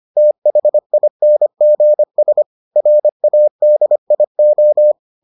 Morse code, heard as a keyed tone, about 25 wpm, 600 Hz. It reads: THINGS RADIO